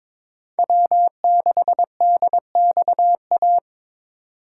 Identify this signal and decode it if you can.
Morse code, heard as a keyed tone, about 26 wpm, 700 Hz. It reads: W6DXA
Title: Speed Racing - USA Call Signs 22wpm